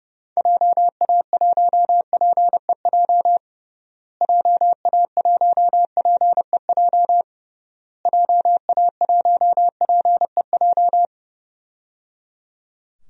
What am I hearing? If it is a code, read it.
Morse code, heard as a keyed tone, about 30 wpm, 700 Hz. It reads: JA1PEJ JA1PEJ JA1PEJ
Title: DXCC-100 Call Signs - 6 Characters - Repeated 3x - 30wpm